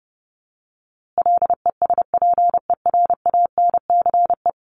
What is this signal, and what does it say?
Morse code, heard as a keyed tone, about 28 wpm, 700 Hz. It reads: LESPERANCE
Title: A Sound Of Thunder 30wpm